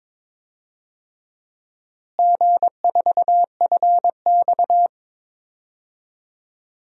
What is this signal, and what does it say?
Morse code, heard as a keyed tone, about 22 wpm, 700 Hz. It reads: G4FX
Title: DXCC-100 Call Signs - 4 Characters - Repeated 3x - 22wpm